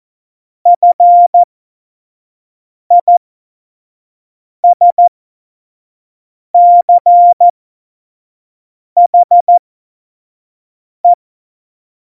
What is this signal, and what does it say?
Morse code, heard as a keyed tone, about 14 wpm, spaced as for 5 wpm, 700 Hz. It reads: FISCHE
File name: Morsekurs / Witze / Witz 26. Juni 2018, 25 F70